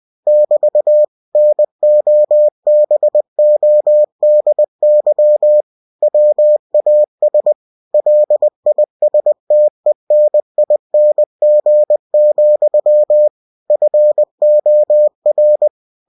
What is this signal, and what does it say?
Morse code, heard as a keyed tone, about 20 wpm, 600 Hz. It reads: <BT> NOBODY WAS LISTENING, FOR
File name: cw-books / winnie-the-pooh / winnie-20wpm-page-0109